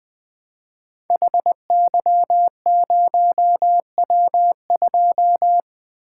Morse code, heard as a keyed tone, about 20 wpm, 700 Hz. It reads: HY0W2
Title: Code Groups - 5 Letter:Numbers - 20wpm